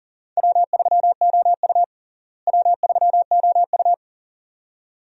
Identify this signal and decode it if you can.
Morse code, heard as a keyed tone, about 40 wpm, 700 Hz. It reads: W3OU W3OU
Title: USA Call Signs - 4 Characters - Repeated 2x - 40wpm